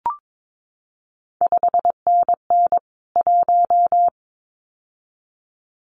Morse code, heard as a keyed tone, about 22 wpm, 700 Hz. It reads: 5NN 1
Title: CQ WW DX Contest Exchange Encore - 22wpm